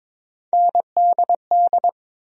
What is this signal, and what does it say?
Morse code, heard as a keyed tone, about 22 wpm, 700 Hz. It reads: NDD
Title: USA Call Sign Postfixes - 3 Letters 22wpm